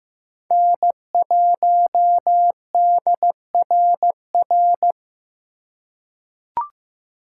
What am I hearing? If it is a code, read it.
Morse code, heard as a keyed tone, about 15 wpm, 700 Hz. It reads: N1DRR
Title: USA Call Signs - 3x Character Spacing - 15wpm